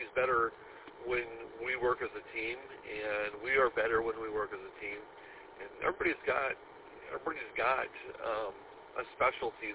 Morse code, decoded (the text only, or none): none